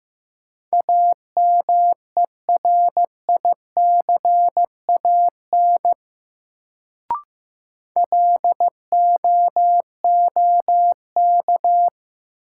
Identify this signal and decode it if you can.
Morse code, heard as a keyed tone, about 15 wpm, 700 Hz. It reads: AMERICAN LOOK
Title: Sets of 2 Words - New Word Drill - 15wpm